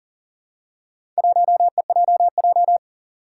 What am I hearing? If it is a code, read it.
Morse code, heard as a keyed tone, about 40 wpm, 700 Hz. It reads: 1EJJ
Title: Four Challenging Letter-Number Combinations 40wpm